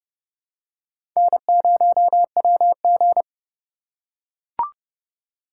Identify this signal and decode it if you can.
Morse code, heard as a keyed tone, about 30 wpm, 700 Hz. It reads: N0WG